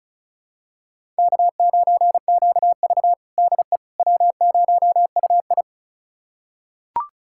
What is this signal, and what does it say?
Morse code, heard as a keyed tone, about 36 wpm, 700 Hz. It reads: K9QV DE W0UI
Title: USA Callsign DE Callsign - 4 Characters - 35wpm